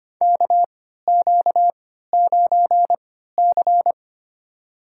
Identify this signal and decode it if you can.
Morse code, heard as a keyed tone, about 25 wpm, spaced as for 14 wpm, 700 Hz. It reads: KQ9C
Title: USA Call Signs - 4 Characters - 3x Character Spacing - 25wpm